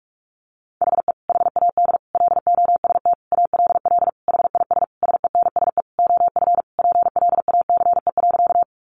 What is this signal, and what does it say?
Morse code, heard as a keyed tone, about 44 wpm, 700 Hz. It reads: HE HAD LOST ALL HIS SENSE OF PLACE.
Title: Sentences from Top 1400 Words 45wpm